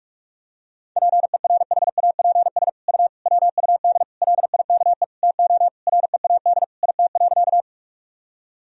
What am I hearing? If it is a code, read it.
Morse code, heard as a keyed tone, about 46 wpm, 700 Hz. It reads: PERHAPS U WUD LIKE TO READ IT.
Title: Sentences from Top 500 Words and Abbreviations 45wpm